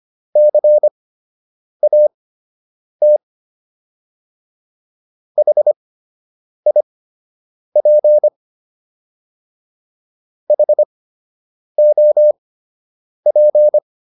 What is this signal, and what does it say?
Morse code, heard as a keyed tone, about 25 wpm, spaced as for 8 wpm, 600 Hz. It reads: CAT HIP HOP